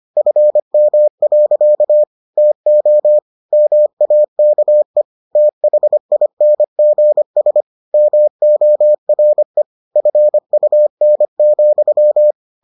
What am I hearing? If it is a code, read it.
Morse code, heard as a keyed tone, about 25 wpm, 600 Hz. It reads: FM. TO MAKE THINGS MORE FUN,